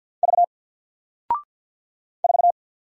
Morse code, heard as a keyed tone, about 48 wpm, 700 Hz. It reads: V 4
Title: Single Letter-Number 50wpm